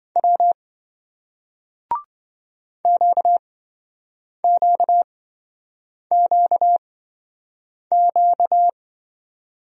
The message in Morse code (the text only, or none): W QQQQ